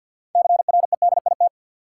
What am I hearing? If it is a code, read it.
Morse code, heard as a keyed tone, about 53 wpm, 700 Hz. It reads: CREDIT
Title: Sentences from Top 1000 Words - New Word Drill - 50wpm